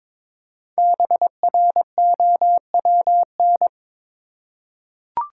Morse code, heard as a keyed tone, about 22 wpm, 700 Hz. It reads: BROWN
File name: 5 Letter Words - First 3 Letters Given - 22wpm